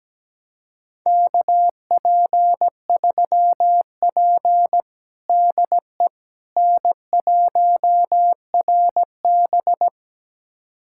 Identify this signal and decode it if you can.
Morse code, heard as a keyed tone, about 17 wpm, 700 Hz. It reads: KP3P DE N1RB
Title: USA Callsign DE Callsign - 4 Characters - 17wpm